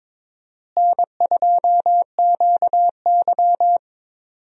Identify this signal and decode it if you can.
Morse code, heard as a keyed tone, about 22 wpm, 700 Hz. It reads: N2QY
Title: USA Call Signs - 4 Characters - Repeated 2x - 22wpm